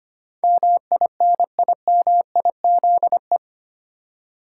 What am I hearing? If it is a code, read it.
Morse code, heard as a keyed tone, about 25 wpm, 690 Hz. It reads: MINIMIZE
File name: 8 Letter Words - First 3 Letters Given - 25wpm